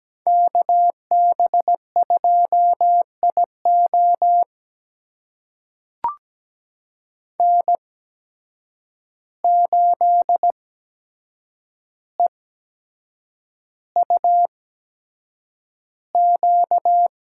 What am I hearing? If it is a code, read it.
Morse code, heard as a keyed tone, about 17 wpm, spaced as for 7 wpm, 700 Hz. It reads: KB2IO N8EUQ
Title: USA Call Signs - 5 Characters - 8x Character Spacing - 17wpm